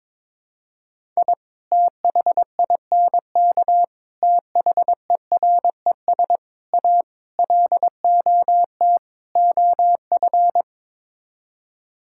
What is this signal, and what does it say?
Morse code, heard as a keyed tone, about 22 wpm, 700 Hz. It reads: I THINK THERES A LOT OF
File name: practice / 22wpm / Sets of 6 Words - 22wpm